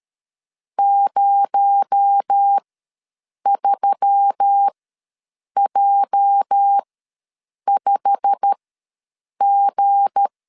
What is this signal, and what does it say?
Morse code, heard as a keyed tone, about 13 wpm, spaced as for 7 wpm, 800 Hz. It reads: 03J5G
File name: LEZIONI / 8 YJ45 / 8 YJ45 ws 12wpm 3146